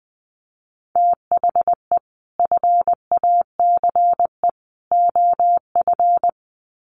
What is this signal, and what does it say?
Morse code, heard as a keyed tone, about 20 wpm, 700 Hz. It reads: THE FACE OF
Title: Sets of 3 Words 20wpm